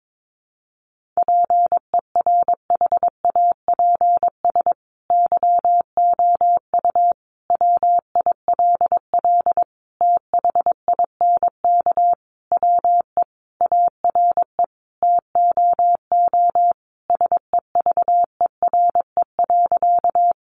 Morse code, heard as a keyed tone, about 22 wpm, 700 Hz. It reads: PERHAPS YOU WILL THINK WE ARE TOO SEVERE.
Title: Sentences from Top 1400 Words 22wpm